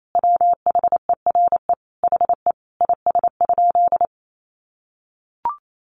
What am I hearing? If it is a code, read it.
Morse code, heard as a keyed tone, about 28 wpm, 700 Hz. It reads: WHERE HE IS?
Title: Sentences from Top 300 Words 28wpm